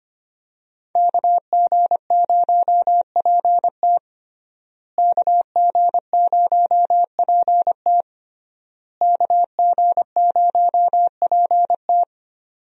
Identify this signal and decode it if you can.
Morse code, heard as a keyed tone, about 25 wpm, 700 Hz. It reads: KG0PT KG0PT KG0PT
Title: USA Call Signs - 5 Characters - Repeated 3x - 25wpm